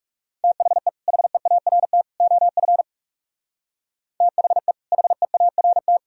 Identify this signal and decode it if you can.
Morse code, heard as a keyed tone, about 44 wpm, 700 Hz. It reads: THE HEART OF THE HEART